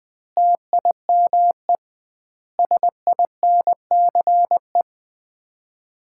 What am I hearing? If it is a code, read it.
Morse code, heard as a keyed tone, about 21 wpm, 700 Hz. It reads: TIME SINCE
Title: 2x Word-Spacing - Sets of 4 Words - 20wpm